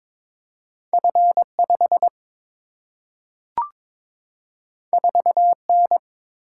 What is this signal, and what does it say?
Morse code, heard as a keyed tone, about 21 wpm, 700 Hz. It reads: F5 4N
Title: Single-Letter Single-Number 22wpm